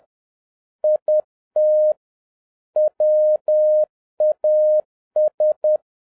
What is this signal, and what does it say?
Morse code, heard as a keyed tone, about 10 wpm, 610 Hz. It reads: IT WAS
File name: 2011-07-mp3 / n46 JOHN K3TN 10wpm